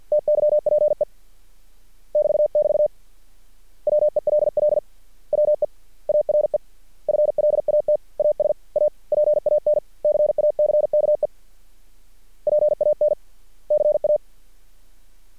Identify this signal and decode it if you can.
Morse code, heard as a keyed tone, about 50 wpm, 600 Hz. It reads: TYPE <BT><BT> WELL WE ARE FLAT AS A PAN CACKE PAN CA
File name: archived / CRSnet